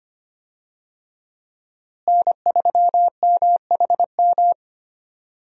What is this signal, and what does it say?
Morse code, heard as a keyed tone, about 25 wpm, 700 Hz. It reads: N3MHM
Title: USA Call Signs - 5 Characters - Repeated 3x - 25wpm